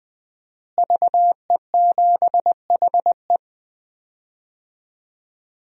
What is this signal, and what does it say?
Morse code, heard as a keyed tone, about 20 wpm, 700 Hz. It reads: VE7HE
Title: DXCC-100 Call Signs - 5 Characters - Repeated 2x - 20wpm